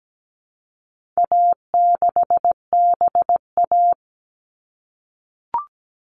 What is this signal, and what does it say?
Morse code, heard as a keyed tone, about 17 wpm, 700 Hz. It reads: A6BA
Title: All Cut Numbers - 4-Digit 17wpm